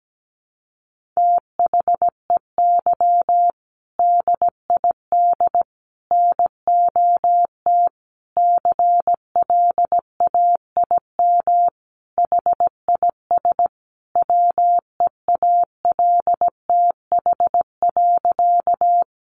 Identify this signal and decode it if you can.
Morse code, heard as a keyed tone, about 17 wpm, 700 Hz. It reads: THEY DID NOT CLAIM HIS WEALTH.
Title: Sentences from Top 1500 Words 17wpm